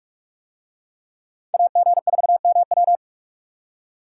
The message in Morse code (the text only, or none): AG4MW